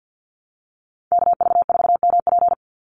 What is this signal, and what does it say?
Morse code, heard as a keyed tone, about 50 wpm, 700 Hz. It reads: KV4MP
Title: USA Call Signs 50wpm